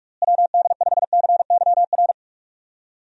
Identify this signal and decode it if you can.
Morse code, heard as a keyed tone, about 46 wpm, 700 Hz. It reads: WD5CYR
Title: USA Call Signs - 6 Characters - Repeated 3x - 45wpm